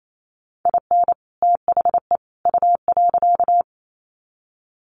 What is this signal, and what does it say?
Morse code, heard as a keyed tone, about 28 wpm, 700 Hz. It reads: IN THE U.
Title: Sets of 3 Words 28wpm